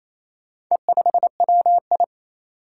Morse code, heard as a keyed tone, about 28 wpm, 700 Hz. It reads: E5WI
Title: Four Challenging Letter-Number Combinations 28wpm